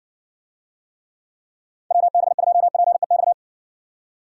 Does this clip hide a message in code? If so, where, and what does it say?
Yes: Morse code from 1.9 to 3.3 s, heard as a keyed tone, about 60 wpm, 700 Hz. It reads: WB2PEX